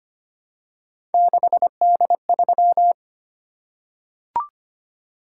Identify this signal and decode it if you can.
Morse code, heard as a keyed tone, about 25 wpm, 700 Hz. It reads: 6D3
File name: Three Letter-Number Combinations - 25wpm